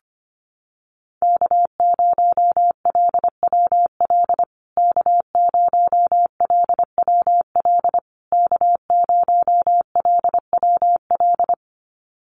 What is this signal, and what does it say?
Morse code, heard as a keyed tone, about 25 wpm, 700 Hz. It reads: K0LWL K0LWL K0LWL
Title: USA Call Signs Repeated 3x 25wpm